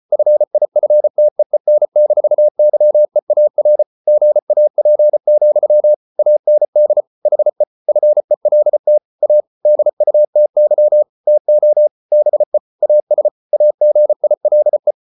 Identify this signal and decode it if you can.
Morse code, heard as a keyed tone, about 34 wpm, 600 Hz. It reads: FIFTEEN-YEAR GAP, AND HE FELT A DUTY TO BE AS AGILE